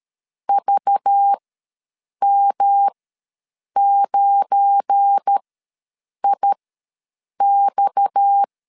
Morse code, heard as a keyed tone, about 13 wpm, spaced as for 7 wpm, 800 Hz. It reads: VM9IX
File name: LEZIONI / 8 YJ45 / 8 Y ws 12wpm 3409